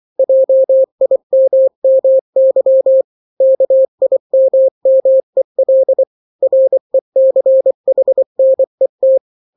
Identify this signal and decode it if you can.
Morse code, heard as a keyed tone, about 24 wpm, 530 Hz. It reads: JIMMY KIMMEL RECHNET
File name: daily / 23wpm